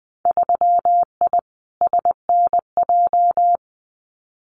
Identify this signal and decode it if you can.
Morse code, heard as a keyed tone, about 20 wpm, 700 Hz. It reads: 3I SNJ
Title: Winter Field Day Exchanges Encore 20wpm